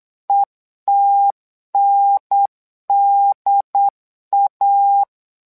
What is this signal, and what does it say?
Morse code, heard as a keyed tone, about 8 wpm, 800 Hz. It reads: ETNDA
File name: LEZIONI / 2 UHDN / 2 UHDN 8wpm2232